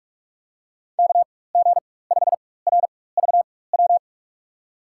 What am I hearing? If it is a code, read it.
Morse code, heard as a keyed tone, about 44 wpm, spaced as for 21 wpm, 690 Hz. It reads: KG5RVW
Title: USA Call Signs - 4x Character Spacing - 45wpm